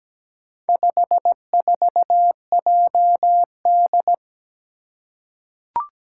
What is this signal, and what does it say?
Morse code, heard as a keyed tone, about 17 wpm, 700 Hz. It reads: 54JD
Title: Four Challenging Letter-Number Combinations 17wpm